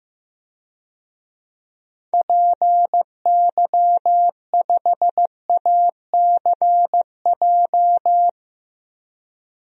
Morse code, heard as a keyed tone, about 15 wpm, 700 Hz. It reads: PY5ACJ
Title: DXCC-100 Call Signs - 6 Characters - 15wpm